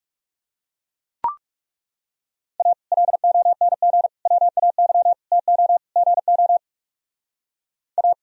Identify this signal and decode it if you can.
Morse code, heard as a keyed tone, about 37 wpm, 700 Hz. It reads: A LONG WAY TO GO A
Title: Speed-Racing - Sets of 5 Words 30wpm